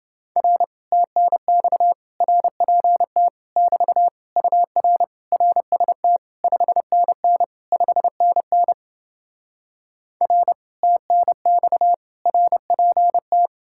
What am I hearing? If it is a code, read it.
Morse code, heard as a keyed tone, about 31 wpm, 700 Hz. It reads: R TNX RPT <BT> UR RST 5NN 5NN R TNX RPT